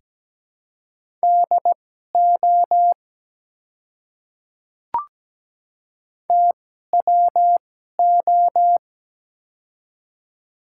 Morse code, heard as a keyed tone, about 17 wpm, spaced as for 12 wpm, 700 Hz. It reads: DO TWO
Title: 2x Character Spacing - Top 200 Words - 17wpm